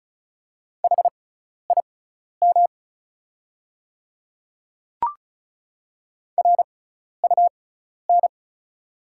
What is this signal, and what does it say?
Morse code, heard as a keyed tone, about 36 wpm, spaced as for 12 wpm, 700 Hz. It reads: HIM RUN
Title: 6x Character Spacing - Top 300 Words - 35wpm